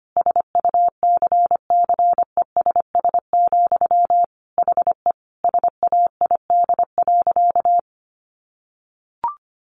Morse code, heard as a keyed tone, about 25 wpm, 700 Hz. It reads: SUCCESS, HE SAID.
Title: Sentences from Top 900 Words 25wpm